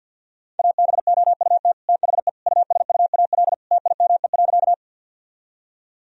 Morse code, heard as a keyed tone, about 52 wpm, 700 Hz. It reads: ABOUT THE USUAL TIME.
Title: Sentences from Top 1000 Words - Review Part 2 - 50wpm